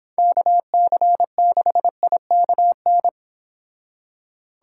Morse code, heard as a keyed tone, about 26 wpm, 700 Hz. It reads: KC6IKN
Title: Speed Racing - USA Call Signs 22wpm